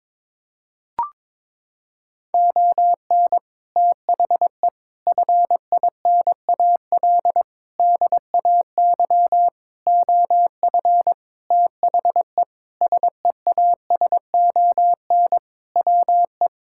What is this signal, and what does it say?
Morse code, heard as a keyed tone, about 22 wpm, 700 Hz. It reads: ON THE FINAL DAY OF THE SEASON WE
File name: Sentences with 7-Word Phrases - 22wpm